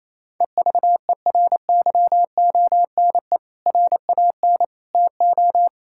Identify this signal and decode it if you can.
Morse code, heard as a keyed tone, about 28 wpm, 700 Hz. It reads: EVERYONE RAN TO
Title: Sentences from Top 1600 Words 28wpm